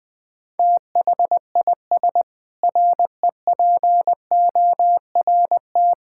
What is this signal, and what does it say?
Morse code, heard as a keyed tone, about 20 wpm, 700 Hz. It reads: THIS REPORT
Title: Sentences with 7-Word Phrases - Spoken Beginning - 20wpm